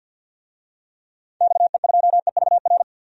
Morse code, heard as a keyed tone, about 45 wpm, 700 Hz. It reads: XE2EVR
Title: DXCC-100 Call Signs - 6 Characters - 50wpm